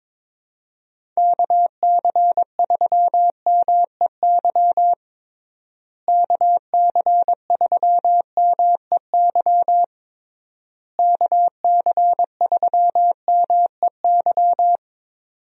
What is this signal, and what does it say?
Morse code, heard as a keyed tone, about 21 wpm, 700 Hz. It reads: KC3MEY KC3MEY KC3MEY
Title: USA Call Signs - 6 Characters - Repeated 3x - 22wpm